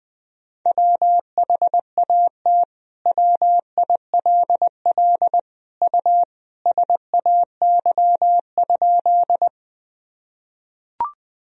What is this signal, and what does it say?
Morse code, heard as a keyed tone, about 20 wpm, 700 Hz. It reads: WHAT WILL U SAY?
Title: Sentences from Top 200 Words and Abbreviations 20wpm